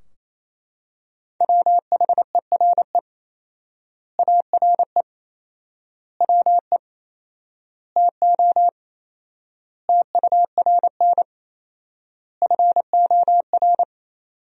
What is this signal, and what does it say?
Morse code, heard as a keyed tone, about 28 wpm, 700 Hz. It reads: WHERE ARE WE TO TURN FOR
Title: 4x Word-Spacing - Sentences from Top 200 Words - 28wpm